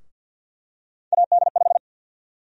Morse code, heard as a keyed tone, about 52 wpm, 700 Hz. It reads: AD5
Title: USA Call Sign Prefixes - Repeated 2x - 50wpm